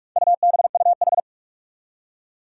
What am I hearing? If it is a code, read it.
Morse code, heard as a keyed tone, about 46 wpm, 700 Hz. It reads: UBUH
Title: Four Challenging Letter-Number Combinations 45wpm